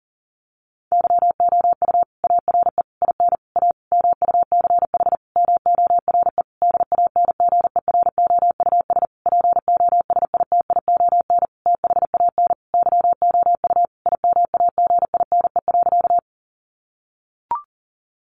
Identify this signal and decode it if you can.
Morse code, heard as a keyed tone, about 40 wpm, 700 Hz. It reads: YOU ARE IN A MUCH MORE DANGEROUS POSITION THAN YOU IMAGINE.
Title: Sentences from Top 1400 Words 40wpm